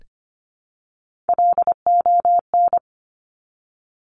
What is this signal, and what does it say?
Morse code, heard as a keyed tone, about 25 wpm, 700 Hz. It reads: LON
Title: Top Three Letter Combinations in Words 25wpm